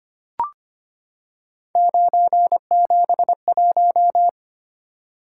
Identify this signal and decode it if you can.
Morse code, heard as a keyed tone, about 25 wpm, 700 Hz. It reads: 971
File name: Three Letter-Number Combinations - 25wpm